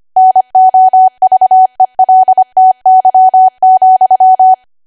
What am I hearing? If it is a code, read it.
Morse code, heard as a keyed tone, about 25 wpm, 750 Hz. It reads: NOVELTY,